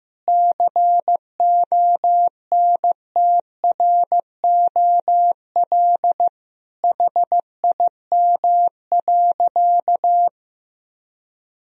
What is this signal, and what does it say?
Morse code, heard as a keyed tone, about 15 wpm, 700 Hz. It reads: CONTROL HIM.
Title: Sentences from Top 1000 Words - Review Part 3 - 15wpm